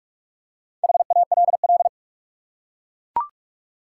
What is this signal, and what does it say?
Morse code, heard as a keyed tone, about 44 wpm, 700 Hz. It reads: HALL
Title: Four Letter Words - 45wpm